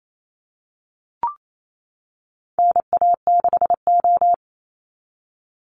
Morse code, heard as a keyed tone, about 28 wpm, 700 Hz. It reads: NA6O
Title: NA SOTA Chaser Callsigns 28wpm